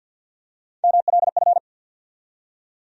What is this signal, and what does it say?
Morse code, heard as a keyed tone, about 52 wpm, 700 Hz. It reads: MLF
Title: USA Call Sign Postfixes - 3 Letters 50wpm